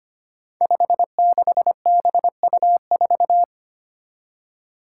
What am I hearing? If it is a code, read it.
Morse code, heard as a keyed tone, about 25 wpm, 700 Hz. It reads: 56BU4
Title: Five Challenging Letter-Number Combinations 25wpm